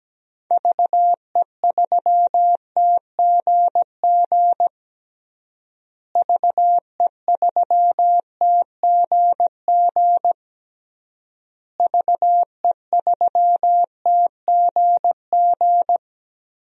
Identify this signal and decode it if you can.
Morse code, heard as a keyed tone, about 17 wpm, 700 Hz. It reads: VE3TGG VE3TGG VE3TGG
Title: DXCC-100 Call Signs - 6 Characters - Repeated 3x - 17wpm